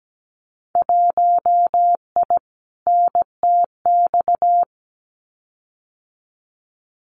Morse code, heard as a keyed tone, about 17 wpm, 700 Hz. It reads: 1I NTX
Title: Winter Field Day Exchanges 17wpm